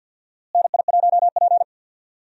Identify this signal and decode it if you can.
Morse code, heard as a keyed tone, about 48 wpm, 700 Hz. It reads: NI1P
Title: USA Call Signs - 4 Characters - Repeated 3x - 50wpm